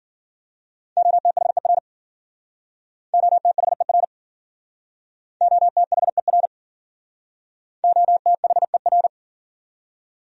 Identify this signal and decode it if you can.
Morse code, heard as a keyed tone, about 53 wpm, 700 Hz. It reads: OTHER OTHER OTHER OTHER